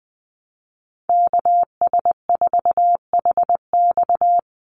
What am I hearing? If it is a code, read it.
Morse code, heard as a keyed tone, about 20 wpm, 700 Hz. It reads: KS4HX
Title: USA Call Signs 20wpm